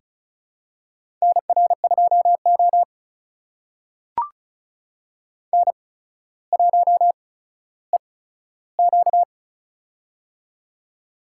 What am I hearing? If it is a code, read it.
Morse code, heard as a keyed tone, about 36 wpm, spaced as for 15 wpm, 700 Hz. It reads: NR2O N1EQ
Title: USA Call Signs - 4 Characters - 8x Character Spacing - 35wpm